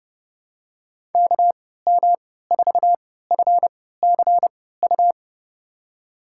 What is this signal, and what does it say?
Morse code, heard as a keyed tone, about 31 wpm, spaced as for 17 wpm, 700 Hz. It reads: KM4FCU
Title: USA Call Signs - 6 Characters - 3x Character Spacing - 30wpm